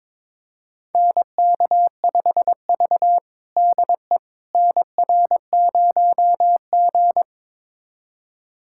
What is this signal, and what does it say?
Morse code, heard as a keyed tone, about 21 wpm, 700 Hz. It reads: NK5V DE NR0G
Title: USA Callsign DE Callsign - 4 Characters - 22wpm